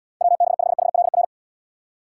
Morse code, heard as a keyed tone, about 78 wpm, 700 Hz. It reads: WB5HLR